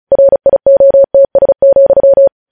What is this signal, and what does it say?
Morse code, heard as a keyed tone, about 36 wpm, 550 Hz. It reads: RIOTS,
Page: news headlines in morse code at 35 wpm